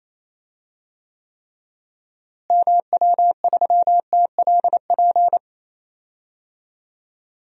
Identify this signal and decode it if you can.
Morse code, heard as a keyed tone, about 27 wpm, 700 Hz. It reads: MW3TLP
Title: DXCC-100 Call Signs - 6 Characters - Repeated 2x - 28wpm